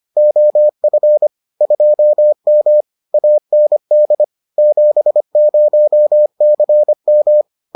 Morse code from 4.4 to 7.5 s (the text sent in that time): 70CM